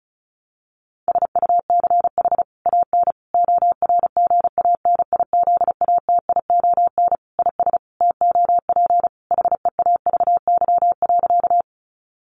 Morse code, heard as a keyed tone, about 34 wpm, 700 Hz. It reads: SUCH AN ORGANIZATION IS TOP HEAVY.
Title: Sentences from Top 1100 Words 35wpm